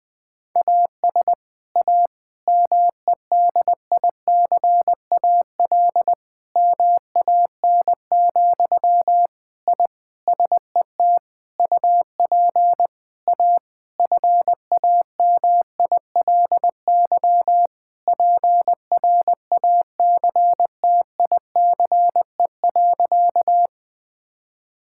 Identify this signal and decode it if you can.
Morse code, heard as a keyed tone, about 21 wpm, 700 Hz. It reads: AS A MEDICAL MAN, I SET UP A FAMILY PRACTICE.
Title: Sentences from Top 1000 Words - Review Part 5 - 20wpm